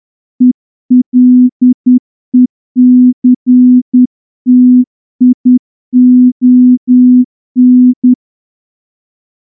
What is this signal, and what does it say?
Morse code, heard as a keyed tone, about 10 wpm, 250 Hz. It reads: ELECTION
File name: morse-training-10WPM